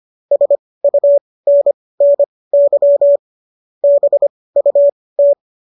Morse code, heard as a keyed tone, about 25 wpm, spaced as for 18 wpm, 570 Hz. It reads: SUNNY BUT